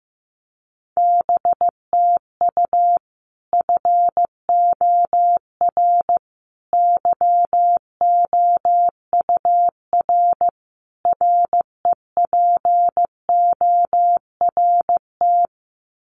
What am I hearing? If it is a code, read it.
Morse code, heard as a keyed tone, about 15 wpm, 700 Hz. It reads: BTU FOR YOUR REPORT
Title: Common QSO Phrases 15wpm